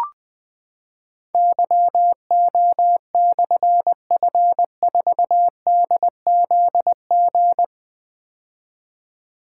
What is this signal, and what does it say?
Morse code, heard as a keyed tone, about 20 wpm, 700 Hz. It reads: YO/F4DZG
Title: Difficult Call Signs 20wpm